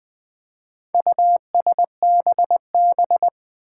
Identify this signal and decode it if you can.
Morse code, heard as a keyed tone, about 20 wpm, 700 Hz. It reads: USBB
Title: Four Challenging Letter-Number Combinations 20wpm